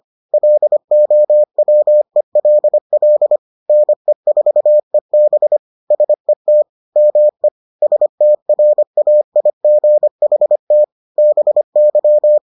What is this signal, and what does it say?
Morse code, heard as a keyed tone, about 25 wpm, 600 Hz. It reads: LOWELL NE4EB SET ME STRAIGHT BY